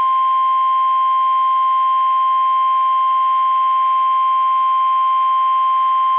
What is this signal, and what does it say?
Emergency Alert System (SAME) tones (the 1050 Hz attention tone). It is heard from 0.6 to 5.4 s.
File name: eas / KUSO / KUSO